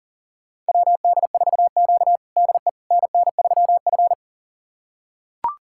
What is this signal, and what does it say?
Morse code, heard as a keyed tone, about 40 wpm, 690 Hz. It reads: WD4Q DE NN3F
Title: USA Callsign DE Callsign - 4 Characters - 40wpm